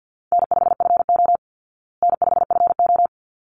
Morse code, heard as a keyed tone, about 50 wpm, 700 Hz. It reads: N5FO N5FO
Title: Call Signs Repeated 3x - 50wpm